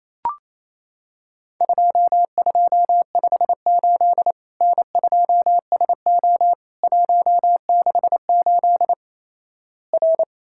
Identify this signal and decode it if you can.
Morse code, heard as a keyed tone, about 28 wpm, 700 Hz. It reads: 2258 N2SO 168 R
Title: Worked All Europe DX Contest - QTC Exchange - 28wpm